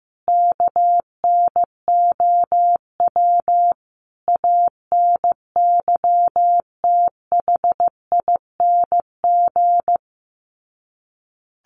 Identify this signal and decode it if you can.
Morse code, heard as a keyed tone, about 15 wpm, 700 Hz. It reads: KNOW ANYTHING
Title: Sets of 5 Words Encore 15wpm